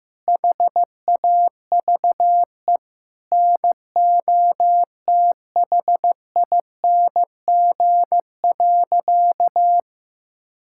Morse code, heard as a keyed tone, about 15 wpm, 700 Hz. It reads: HAVE NOTHING.
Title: Sentences from Top 1000 Words - Review Part 6 - 15wpm